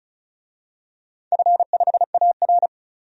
Morse code, heard as a keyed tone, about 35 wpm, 700 Hz. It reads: F5AR
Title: DXCC-100 Call Signs - 4 Characters - 35wpm